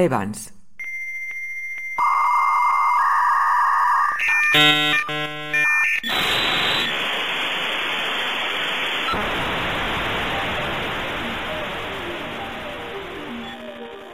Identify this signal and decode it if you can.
Dial-up modem handshake, with speech.